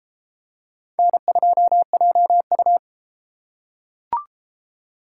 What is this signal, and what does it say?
Morse code, heard as a keyed tone, about 30 wpm, 700 Hz. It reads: N2JU